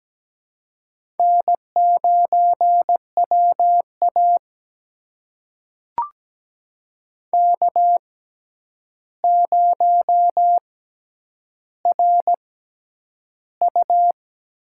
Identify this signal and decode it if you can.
Morse code, heard as a keyed tone, about 17 wpm, spaced as for 9 wpm, 700 Hz. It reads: N9WA K0RU
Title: USA Call Signs - 4 Characters - 6x Character Spacing - 17wpm